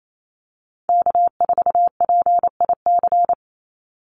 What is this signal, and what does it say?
Morse code, heard as a keyed tone, about 28 wpm, 700 Hz. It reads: K4PIC
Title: NA SOTA Chaser Callsigns 28wpm